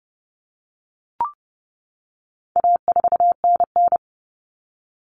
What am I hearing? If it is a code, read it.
Morse code, heard as a keyed tone, about 30 wpm, 700 Hz. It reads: A4NN
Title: Common Cut Numbers - 4-Digit 30wpm